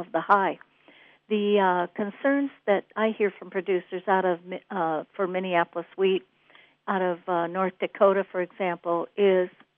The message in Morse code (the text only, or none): none